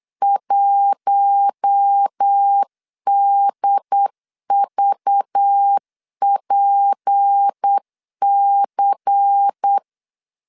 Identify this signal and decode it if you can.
Morse code, heard as a keyed tone, about 8 wpm, 800 Hz. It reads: 1DVPC
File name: LEZIONI / 6 PKX0 / 6 P 8wpm 3025